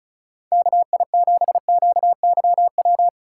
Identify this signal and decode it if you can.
Morse code, heard as a keyed tone, about 37 wpm, 700 Hz. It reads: KI7QYW
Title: Speed Racing - USA 2x3 Call Signs 30wpm